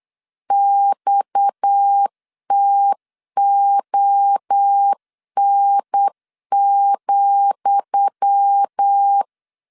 Morse code, heard as a keyed tone, about 8 wpm, 800 Hz. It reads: XTON,
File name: LEZIONI / 10 SIGNS / 10D COMA 8wpm 3749